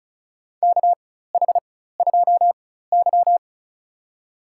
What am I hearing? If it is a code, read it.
Morse code, heard as a keyed tone, about 34 wpm, spaced as for 16 wpm, 700 Hz. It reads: KH2Y
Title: USA Call Signs - 4 Characters - 4x Character Spacing - 35wpm